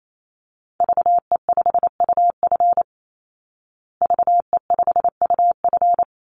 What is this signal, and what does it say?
Morse code, heard as a keyed tone, about 28 wpm, 700 Hz. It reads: VE5UF VE5UF
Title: Call Signs Repeated 2x - 28wpm